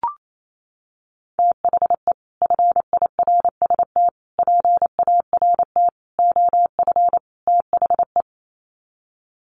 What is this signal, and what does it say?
Morse code, heard as a keyed tone, about 28 wpm, 700 Hz. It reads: THE FIRST PART OF THE
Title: Sets of 5 Words Encore 28wpm